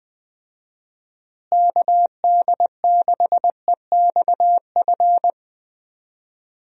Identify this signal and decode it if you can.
Morse code, heard as a keyed tone, about 20 wpm, 700 Hz. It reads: KD6EXF
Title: USA Call Signs - 6 Characters - Repeated 3x - 20wpm